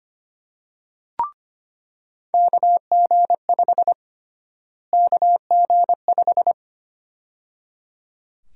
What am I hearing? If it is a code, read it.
Morse code, heard as a keyed tone, about 25 wpm, 700 Hz. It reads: KG5 KG5
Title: USA Call Sign Prefixes - Repeated 2x - 25wpm